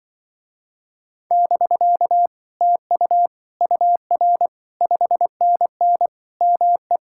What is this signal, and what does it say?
Morse code, heard as a keyed tone, about 24 wpm, 700 Hz. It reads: <BK> TU UR 5NN ME
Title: Speed-Racing Common QSO Phrases Encore - 20wpm